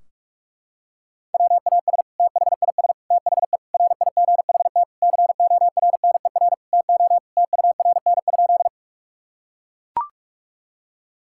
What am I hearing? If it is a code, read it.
Morse code, heard as a keyed tone, about 42 wpm, 700 Hz. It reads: WAS THIS THE RIGHT CORNER TO TURN?
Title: Sentences from Top 1600 Words 45wpm